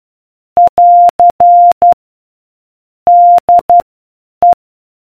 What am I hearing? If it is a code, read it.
Morse code, heard as a keyed tone, about 12 wpm, spaced as for 8 wpm, 700 Hz. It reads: <AR> DE